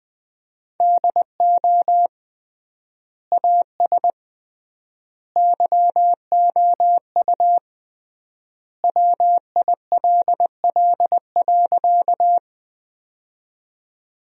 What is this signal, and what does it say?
Morse code, heard as a keyed tone, about 21 wpm, 700 Hz. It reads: DO AS YOU WILL.
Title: 3x Word-Spacing - Sentences from Top 100 Words - 20wpm